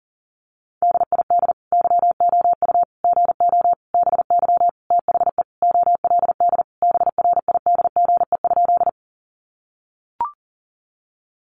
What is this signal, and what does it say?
Morse code, heard as a keyed tone, about 40 wpm, 700 Hz. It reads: DID YOU GO BY THE OLD BRIDGE?
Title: Sentences from Top 1200 Words 40wpm